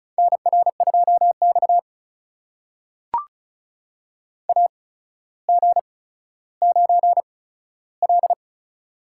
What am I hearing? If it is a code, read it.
Morse code, heard as a keyed tone, about 36 wpm, spaced as for 15 wpm, 700 Hz. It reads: NR2X AG9L
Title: USA Call Signs - 4 Characters - 8x Character Spacing - 35wpm